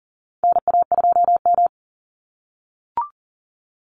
Morse code, heard as a keyed tone, about 40 wpm, 700 Hz. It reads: NA2M
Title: Call Signs - 40wpm